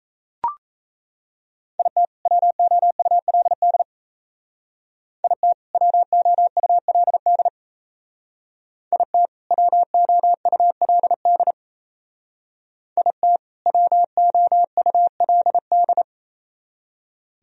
Morse code, heard as a keyed tone, about 36 wpm, 700 Hz. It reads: IT WOULD IT WOULD IT WOULD IT WOULD